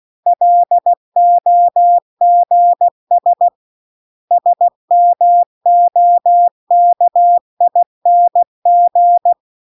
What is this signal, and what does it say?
Morse code, heard as a keyed tone, about 16 wpm, 700 Hz. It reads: LOGS SMOKING
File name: CW-The-Big-Twoheaded-River-2-16WPM